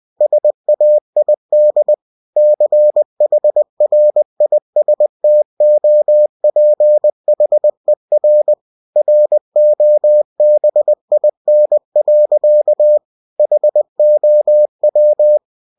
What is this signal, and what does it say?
Morse code, heard as a keyed tone, about 20 wpm, 600 Hz. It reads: SAID CHRISTOPHER ROBIN. HOW
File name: cw-books / winnie-the-pooh / winnie-20wpm-page-0044